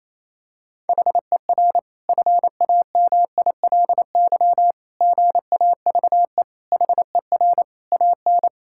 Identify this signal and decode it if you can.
Morse code, heard as a keyed tone, about 28 wpm, 700 Hz. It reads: HER FAMILY GAVE HER AN
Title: Sentences from Top 1000 Words - Review Part 5 - 28wpm